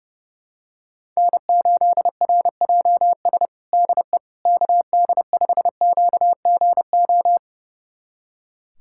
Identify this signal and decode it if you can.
Morse code, heard as a keyed tone, about 32 wpm, 700 Hz. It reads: N8RJS DE KD5QGO
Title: USA Callsign DE Callsign 30wpm